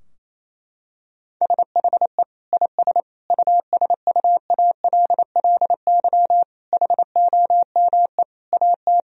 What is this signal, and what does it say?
Morse code, heard as a keyed tone, about 28 wpm, 700 Hz. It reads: SHE IS USUALLY HOME AT
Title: Sentences from Top 1000 Words - Review Part 3 - 28wpm